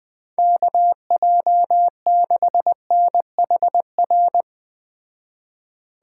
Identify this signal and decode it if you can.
Morse code, heard as a keyed tone, about 20 wpm, 700 Hz. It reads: KJ6NHR